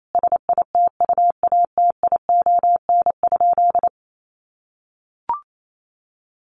Morse code, heard as a keyed tone, about 28 wpm, 700 Hz. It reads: SITUATION?
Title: Sentences from Top 1200 Words 28wpm